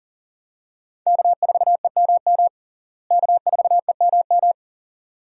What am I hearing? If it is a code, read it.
Morse code, heard as a keyed tone, about 40 wpm, 700 Hz. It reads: K4EMM K4EMM
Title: USA Call Signs - 5 Characters - Repeated 2x - 40wpm